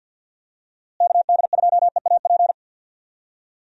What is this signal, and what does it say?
Morse code, heard as a keyed tone, about 46 wpm, 700 Hz. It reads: KD2EAP
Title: USA Call Signs - 6 Characters - Repeated 2x - 50wpm